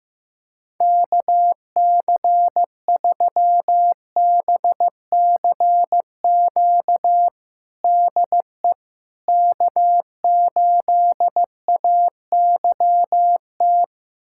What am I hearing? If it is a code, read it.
Morse code, heard as a keyed tone, about 15 wpm, 700 Hz. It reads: KC3BCQ DE K8AYT
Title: USA Callsign DE Callsign 15wpm